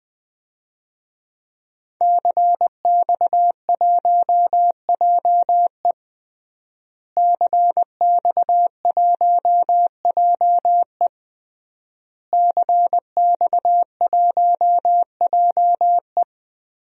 Morse code, heard as a keyed tone, about 20 wpm, 700 Hz. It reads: CX1JE CX1JE CX1JE
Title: DXCC-100 Call Signs - 5 Characters - Repeated 3x - 20wpm